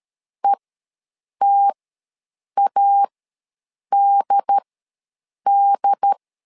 Morse code, heard as a keyed tone, about 13 wpm, spaced as for 7 wpm, 800 Hz. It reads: ETADD
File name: LEZIONI / 2 UHDN / 2 UHD ws 12wpm 1722